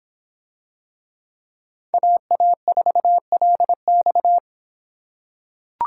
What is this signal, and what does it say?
Morse code, heard as a keyed tone, about 26 wpm, 700 Hz. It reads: AA4LX